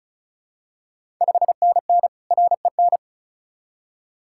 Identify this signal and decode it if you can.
Morse code, heard as a keyed tone, about 36 wpm, 700 Hz. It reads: 5NN REN
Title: Ontario QSO Party Exchange 35wpm